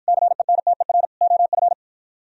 Morse code, heard as a keyed tone, about 53 wpm, 700 Hz. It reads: CENTER OF